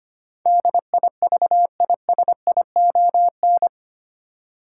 Morse code, heard as a keyed tone, about 25 wpm, 700 Hz. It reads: DIVISION
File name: 8 Letter Words - First 3 Letters Given - 25wpm